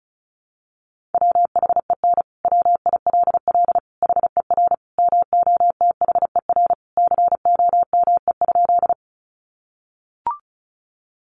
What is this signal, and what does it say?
Morse code, heard as a keyed tone, about 34 wpm, 700 Hz. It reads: WHEN WILL HER MOTHER COME?
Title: Sentences from Top 200 Words 35wpm